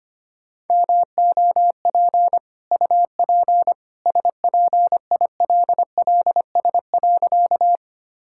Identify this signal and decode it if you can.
Morse code, heard as a keyed tone, about 25 wpm, 700 Hz. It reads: MOP UP SPILLS.
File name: Easy Sentences with Hints - 25wpm